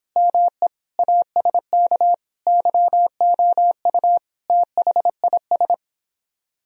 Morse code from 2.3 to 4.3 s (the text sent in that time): YOU